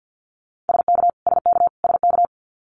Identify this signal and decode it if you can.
Morse code, heard as a keyed tone, about 50 wpm, 700 Hz. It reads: SK SK SK
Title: Canadian Provinces and Territories - Repeated 3x - 50wpm